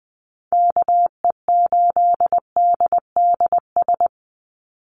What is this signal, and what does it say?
Morse code, heard as a keyed tone, about 20 wpm, 700 Hz. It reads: KE8DDS
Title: USA Call Signs 20wpm